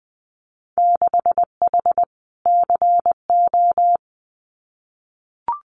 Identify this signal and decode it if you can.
Morse code, heard as a keyed tone, about 20 wpm, 700 Hz. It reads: 6H CO
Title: Winter Field Day Exchanges Encore 20wpm